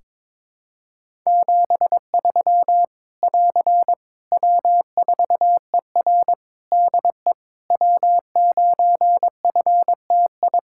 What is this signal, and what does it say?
Morse code, heard as a keyed tone, about 22 wpm, 700 Hz. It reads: 73 <AR> W4ER DE W9FTI
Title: Practice QSO - Exchange 4 - Ending - 22wpm